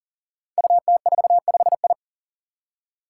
Morse code, heard as a keyed tone, about 39 wpm, 700 Hz. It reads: UT45I
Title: Five Challenging Letter-Number Combinations 40wpm